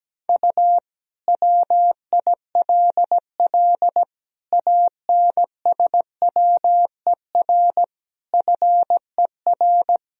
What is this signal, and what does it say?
Morse code, heard as a keyed tone, about 17 wpm, 700 Hz. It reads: U WILL ANSWER FER
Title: Sentences from Top 300 Words and Abbreviations 17wpm